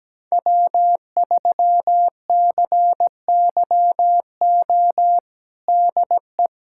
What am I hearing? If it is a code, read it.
Morse code, heard as a keyed tone, about 17 wpm, 700 Hz. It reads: W3CYO DE
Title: USA Callsign DE Callsign - 5 Characters - 17wpm